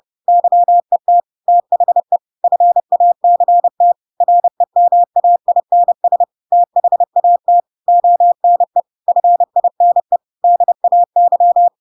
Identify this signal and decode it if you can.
Morse code, heard as a keyed tone, about 30 wpm, 700 Hz. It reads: YET THE FACT REMAINS THAT ONE FINE DAY
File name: Pinocchio / Pinocchio ch1 30 30 ik1hge